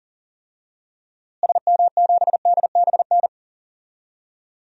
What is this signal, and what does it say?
Morse code, heard as a keyed tone, about 40 wpm, 700 Hz. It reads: SM7DBN
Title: DXCC-100 Call Signs - 6 Characters - Repeated 3x - 40wpm